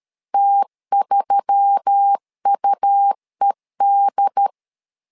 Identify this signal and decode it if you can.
Morse code, heard as a keyed tone, about 13 wpm, 800 Hz. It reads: T3UED
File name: LEZIONI / 5 LBVC / 5 LB 12wpm 1906